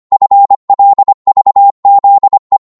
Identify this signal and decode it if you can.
Morse code, heard as a keyed tone, about 25 wpm, 800 Hz. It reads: FLVZE
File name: Morse / ltr-25wpm